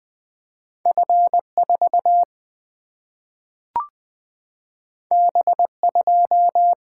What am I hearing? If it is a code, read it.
Morse code, heard as a keyed tone, about 20 wpm, 700 Hz. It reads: F4 B2
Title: Single-Letter Single-Number 20wpm